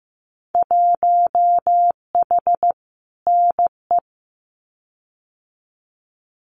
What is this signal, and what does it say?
Morse code, heard as a keyed tone, about 15 wpm, 700 Hz. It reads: 1H NE
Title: Winter Field Day Exchanges 15wpm